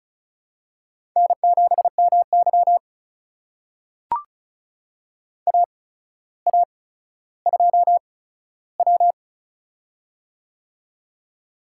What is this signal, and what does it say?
Morse code, heard as a keyed tone, about 36 wpm, spaced as for 15 wpm, 700 Hz. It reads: N7MY AA2W
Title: USA Call Signs - 4 Characters - 8x Character Spacing - 35wpm